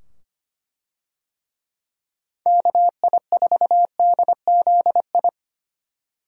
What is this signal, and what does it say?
Morse code, heard as a keyed tone, about 25 wpm, 700 Hz. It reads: KI4DZI
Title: USA Call Signs - 6 Characters - 25wpm